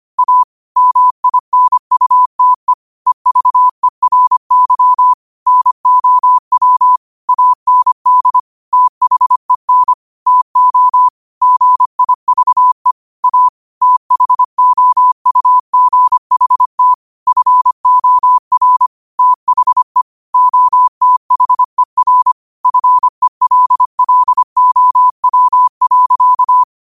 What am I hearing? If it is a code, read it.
Morse code, heard as a keyed tone, about 25 wpm, 1000 Hz. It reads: A MINUTE EVERY NOW AND THEN TO GIVE A THOUGHT FOR THE OTHER FELLOW.
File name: qotdfiles / 25wpm